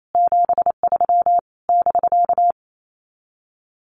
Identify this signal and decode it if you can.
Morse code, heard as a keyed tone, about 28 wpm, 700 Hz. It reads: 73 <BK>
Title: Common QSO Phrases Encore 28wpm